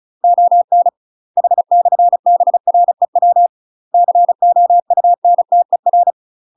Morse code, heard as a keyed tone, about 36 wpm, 700 Hz. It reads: ON H/BREW COUNTER
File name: UK MASTER QSOs TRAINING / 35WPM / UK MASTER QSOs 0018